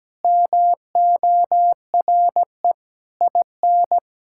Morse code, heard as a keyed tone, about 17 wpm, 700 Hz. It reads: MORE IN
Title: Sentences from Top 1000 Words - Review Part 5 - 17wpm